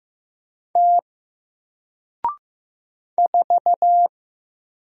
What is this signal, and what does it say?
Morse code, heard as a keyed tone, about 15 wpm, 700 Hz. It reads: T 4
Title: Single Letter-Number Alt 15wpm